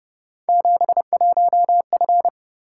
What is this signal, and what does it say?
Morse code, heard as a keyed tone, about 29 wpm, 700 Hz. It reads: 71F
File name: Speed-Racing - Three Letter-Number Combinations - 25wpm